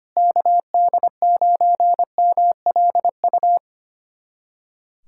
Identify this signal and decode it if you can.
Morse code, heard as a keyed tone, about 25 wpm, 700 Hz. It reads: KD9MLU
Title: USA Callsign DE Callsign - 6 Characters - 25wpm